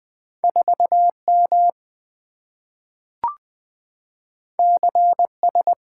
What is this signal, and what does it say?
Morse code, heard as a keyed tone, about 20 wpm, 700 Hz. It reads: 4M CS
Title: Two Letter-Number Combinations - 20wpm